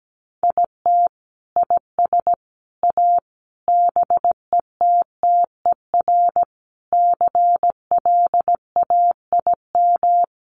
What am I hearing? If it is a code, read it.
Morse code, heard as a keyed tone, about 17 wpm, 700 Hz. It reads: IT IS A BETTER CLAIM
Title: Sentences from Top 1500 Words 17wpm